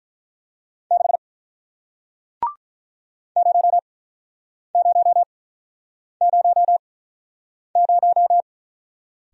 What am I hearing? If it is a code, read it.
Morse code, heard as a keyed tone, about 50 wpm, spaced as for 9 wpm, 700 Hz. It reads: 6 0000